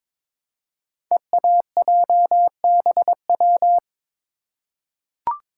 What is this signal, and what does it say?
Morse code, heard as a keyed tone, about 22 wpm, 700 Hz. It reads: EAJBW
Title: Code Groups - 5 Letters - 22wpm